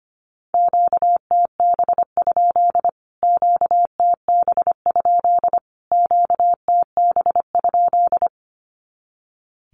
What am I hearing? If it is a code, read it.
Morse code, heard as a keyed tone, about 25 wpm, 700 Hz. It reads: QTB? QTB? QTB?
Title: Q-Codes Repeated 3x 25wpm